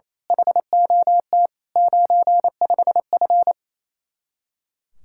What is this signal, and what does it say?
Morse code, heard as a keyed tone, about 27 wpm, 700 Hz. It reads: HOT 95F
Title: Speed-Racing Common QSO Phrases - 28wpm